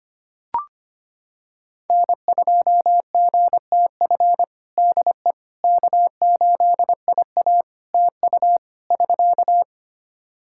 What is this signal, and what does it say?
Morse code, heard as a keyed tone, about 25 wpm, 700 Hz. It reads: N2GTF DE K8IA TU <SK>
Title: Practice QSO - Exchange 4 - Ending - 25wpm